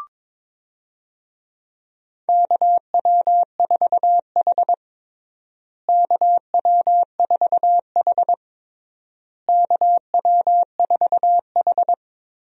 Morse code, heard as a keyed tone, about 22 wpm, 700 Hz. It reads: KW4H KW4H KW4H
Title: USA Call Signs - 4 Characters - Repeated 3x - 22wpm